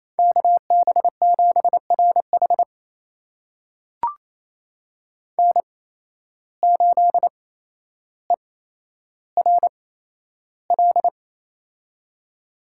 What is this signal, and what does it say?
Morse code, heard as a keyed tone, about 28 wpm, spaced as for 12 wpm, 700 Hz. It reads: KB7RH N8ERL
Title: USA Call Signs - 8x Character Spacing - 28wpm